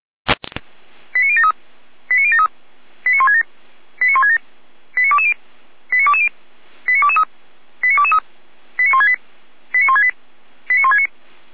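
Five-tone selective call (ZVEI), heard as quick runs of five tones, tones of about 70 ms. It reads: 89073 89073 89167 89167 89220 89220 89223 89223 89168 89168 89169
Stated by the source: ZVEI five-tone selective call 89060 received via SDR.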